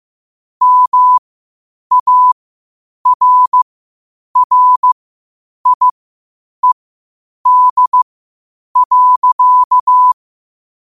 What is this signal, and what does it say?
Morse code, heard as a keyed tone, about 15 wpm, spaced as for 8 wpm, 1000 Hz. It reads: MARRIED.
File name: qotdfiles / 8wpm